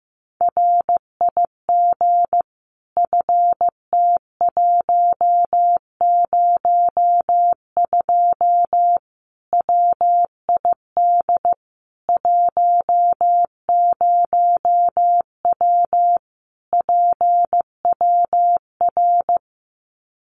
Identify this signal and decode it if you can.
Morse code, heard as a keyed tone, about 15 wpm, 700 Hz. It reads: RIG FT102 WID 10W PWR
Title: Common QSO Phrases 15wpm